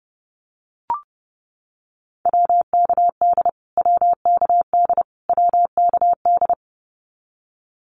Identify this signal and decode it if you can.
Morse code, heard as a keyed tone, about 30 wpm, 700 Hz. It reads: WKD WKD WKD
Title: Abbreviations Repeated 3x - 30wpm